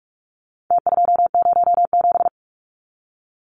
Morse code, heard as a keyed tone, about 44 wpm, 700 Hz. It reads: T207
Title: Numbers - Contest Style - 4-Digit 45wpm